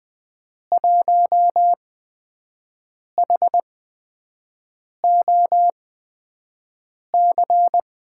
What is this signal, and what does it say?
Morse code, heard as a keyed tone, about 20 wpm, spaced as for 5 wpm, 700 Hz. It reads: 1HOC
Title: USA Call Signs - 5 Characters - 8x Character Spacing - 20wpm